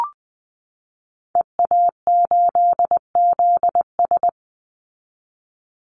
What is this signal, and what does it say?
Morse code, heard as a keyed tone, about 20 wpm, 700 Hz. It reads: EA8ZS
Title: Call Signs Repeated 2x - 20wpm